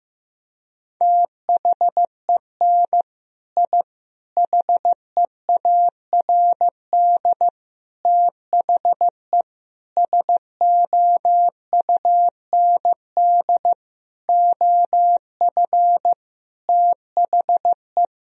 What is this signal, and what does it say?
Morse code, heard as a keyed tone, about 16 wpm, 700 Hz. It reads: THEN I HEARD THE SOUND OF THE
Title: Sentences from Top 1000 Words - Review Part 1 - 15wpm